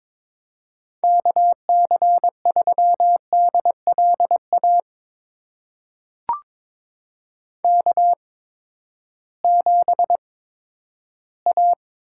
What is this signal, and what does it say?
Morse code, heard as a keyed tone, about 22 wpm, spaced as for 12 wpm, 700 Hz. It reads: KC3DLA K7A
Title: USA Call Signs - 8x Character Spacing - 22wpm